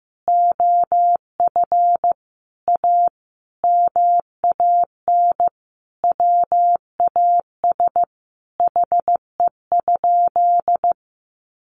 Morse code, heard as a keyed tone, about 15 wpm, 700 Hz. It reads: OF A MAN WAS HE?
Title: Sentences from Top 1300 Words 15wpm